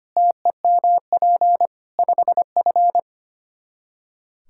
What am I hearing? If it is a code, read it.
Morse code, heard as a keyed tone, about 25 wpm, 700 Hz. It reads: TEMP 5F
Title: Practice QSO - Exchange 2 - Info - 25wpm